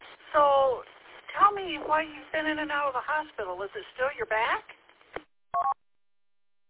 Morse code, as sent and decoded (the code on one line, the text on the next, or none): none